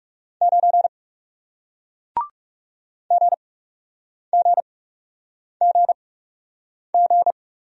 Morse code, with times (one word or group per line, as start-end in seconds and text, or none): 0.4-0.9 9
3.1-7.3 GGGG